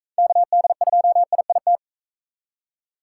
Morse code, heard as a keyed tone, about 39 wpm, 700 Hz. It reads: KD2IIT